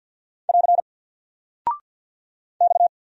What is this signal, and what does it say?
Morse code, heard as a keyed tone, about 48 wpm, 700 Hz. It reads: <AR> X
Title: Single Character (Letter:Number:Punctuation:Prosign) Alt 50wpm